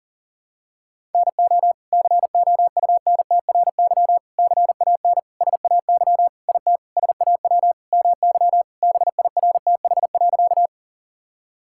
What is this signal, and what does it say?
Morse code, heard as a keyed tone, about 40 wpm, 700 Hz. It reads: NO COUNTRY CAN SAY IT SAW MY BIRTH.
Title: Sentences from Top 1000 Words - Review Part 5 - 40wpm